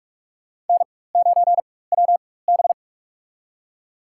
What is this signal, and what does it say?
Morse code, heard as a keyed tone, about 47 wpm, spaced as for 21 wpm, 700 Hz. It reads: N9WB
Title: USA Call Signs - 4 Characters - 4x Character Spacing - 45wpm